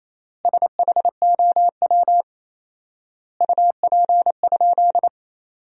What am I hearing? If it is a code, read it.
Morse code, heard as a keyed tone, about 28 wpm, 700 Hz. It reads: SHOW UP?
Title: 4x Word-Spacing - Sentences from Top 200 Words - 28wpm